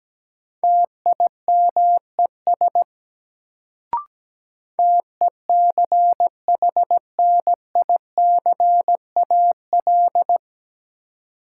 Morse code, heard as a keyed tone, about 17 wpm, 700 Hz. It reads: TIMES TECHNICAL
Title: Sentences from Top 1100 Words - New Word Drill Alt - 17wpm